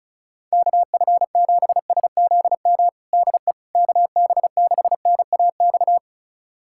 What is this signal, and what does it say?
Morse code, heard as a keyed tone, about 34 wpm, 700 Hz. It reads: KF7SZM DE KB6NAX
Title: USA Callsign DE Callsign - 6 Characters - 35wpm